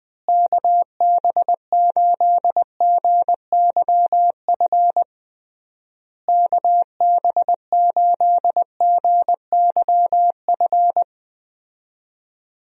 Morse code, heard as a keyed tone, about 20 wpm, 700 Hz. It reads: KB8GYF KB8GYF
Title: USA Call Signs - 6 Characters - Repeated 3x - 20wpm